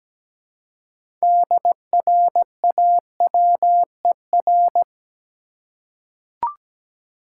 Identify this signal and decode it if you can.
Morse code, heard as a keyed tone, about 17 wpm, 700 Hz. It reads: DRAWER
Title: Six Letter Words - 17wpm